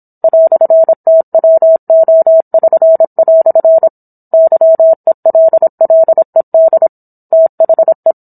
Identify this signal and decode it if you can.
Morse code, heard as a keyed tone, about 27 wpm, 650 Hz. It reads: "TWO<SN>" YELLED THE